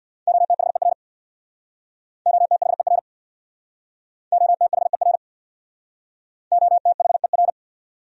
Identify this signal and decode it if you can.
Morse code, heard as a keyed tone, about 65 wpm, 700 Hz. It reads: OTHER OTHER OTHER OTHER